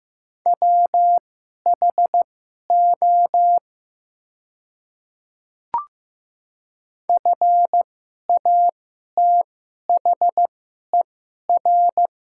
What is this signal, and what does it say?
Morse code, heard as a keyed tone, about 15 wpm, spaced as for 11 wpm, 700 Hz. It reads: WHO FATHER
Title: 2x Character Spacing - Top 300 Words - 15wpm